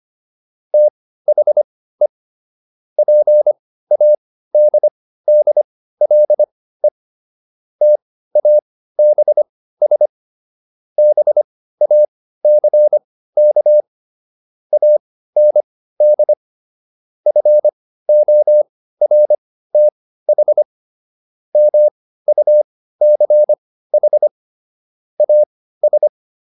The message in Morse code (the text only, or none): THE PADDLE TABS BACK AND FORTH MUCH AS